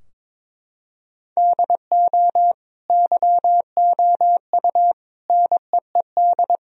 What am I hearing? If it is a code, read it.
Morse code, heard as a keyed tone, about 22 wpm, 700 Hz. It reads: DO YOU NEED
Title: Sentences from Top 1000 Words - Review Part 4 - 22wpm